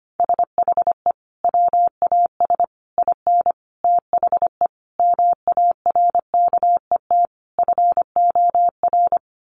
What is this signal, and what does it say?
Morse code, heard as a keyed tone, about 25 wpm, 700 Hz. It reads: SHE WAS IN THE MARKET FOR
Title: Sentences from Top 600 Words 25wpm